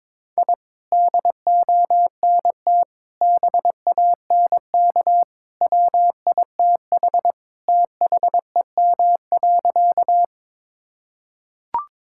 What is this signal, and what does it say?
Morse code, heard as a keyed tone, about 22 wpm, 700 Hz. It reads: I DONT BANK WITH THEM.
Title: Sentences from Top 1000 Words - Review Part 6 - 22wpm